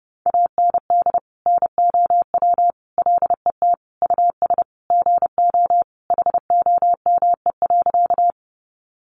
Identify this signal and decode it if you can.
Morse code, heard as a keyed tone, about 30 wpm, 700 Hz. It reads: AND NOW LET US GO HOME.
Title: Sentences from Top 300 Words 30wpm